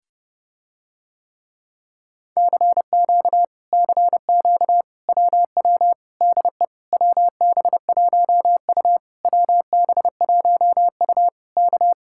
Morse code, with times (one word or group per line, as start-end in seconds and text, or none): 2.4-3.4 CQ
3.7-4.8 CQ
5.1-5.9 WW
6.2-6.6 DE
6.9-9.0 WB1U
9.2-11.3 WB1U
11.6-11.9 K